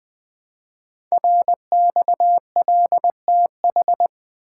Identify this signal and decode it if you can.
Morse code, heard as a keyed tone, about 20 wpm, 700 Hz. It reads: RXLTH
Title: Code Groups - 5 Letters - 20wpm